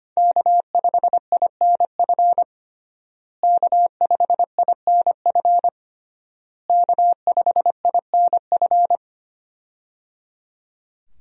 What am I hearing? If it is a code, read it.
Morse code, heard as a keyed tone, about 25 wpm, 700 Hz. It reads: K5INF K5INF K5INF
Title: USA Call Signs - 5 Characters - Repeated 3x - 25wpm